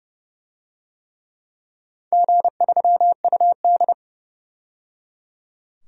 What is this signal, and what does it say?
Morse code, heard as a keyed tone, about 30 wpm, 700 Hz. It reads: G3UD
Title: DXCC-100 Call Signs - 4 Characters - 30wpm